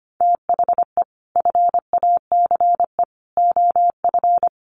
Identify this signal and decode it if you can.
Morse code, heard as a keyed tone, about 25 wpm, 700 Hz. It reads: THE FACE OF
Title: Sets of 3 Words 25wpm